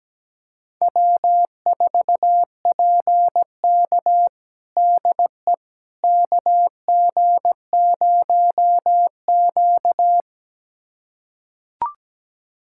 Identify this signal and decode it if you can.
Morse code, heard as a keyed tone, about 17 wpm, 700 Hz. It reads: W4PK DE KG0Q
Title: USA Callsign DE Callsign - 4 Characters - 17wpm